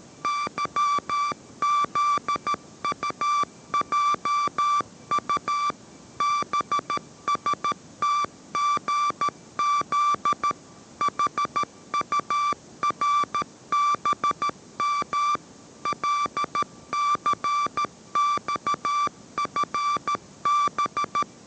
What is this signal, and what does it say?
Morse code, heard as a keyed tone, about 13 wpm, 1170 Hz. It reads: YZUJU BSTGZ HURBM LCXFB